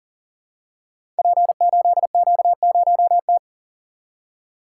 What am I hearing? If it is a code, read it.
Morse code, heard as a keyed tone, about 41 wpm, 700 Hz. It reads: P8Q0T
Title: Code Groups - 5 Letter:Numbers - 40wpm